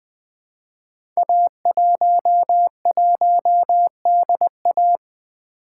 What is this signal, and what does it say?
Morse code, heard as a keyed tone, about 20 wpm, 700 Hz. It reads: A11DA
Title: Code Groups - 5 Letter:Numbers - 20wpm